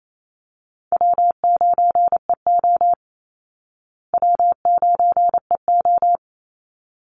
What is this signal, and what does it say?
Morse code, heard as a keyed tone, about 28 wpm, 700 Hz. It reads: W9EO W9EO
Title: Call Signs Repeated 3x - 28wpm